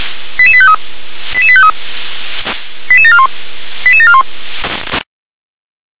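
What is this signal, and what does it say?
Five-tone selective call (ZVEI), heard as quick runs of five tones, tones of about 70 ms. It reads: 88853 88853 88741 88741
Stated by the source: Five-tone ZVEI selective call 88801 received on an SDR.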